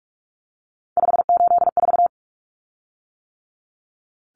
Morse code, heard as a keyed tone, about 44 wpm, 700 Hz. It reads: 584
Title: Common Cut Numbers - 3-Digit 45wpm